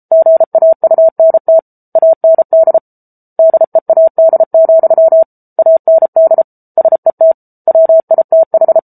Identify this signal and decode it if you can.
Morse code, heard as a keyed tone, about 34 wpm, 650 Hz. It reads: GAUNT AND DEAD, AND SET WITH
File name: Book Two / 07-Chapter 7 - 33 WPM